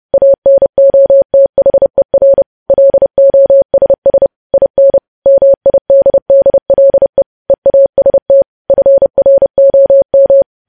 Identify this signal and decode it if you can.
Morse code, heard as a keyed tone, about 30 wpm, 550 Hz. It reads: ANOTHER LOSS IN MIDDLE EAST FROM
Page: news headlines in morse code at 30 wpm